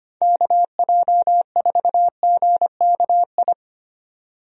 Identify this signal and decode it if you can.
Morse code, heard as a keyed tone, about 25 wpm, 700 Hz. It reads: KJ4GKI
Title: USA Call Signs - 6 Characters - 25wpm